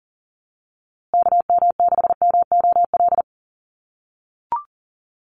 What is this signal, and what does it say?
Morse code, heard as a keyed tone, about 40 wpm, 700 Hz. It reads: KM6MOL
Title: USA Call Signs Repeated 2x 40wpm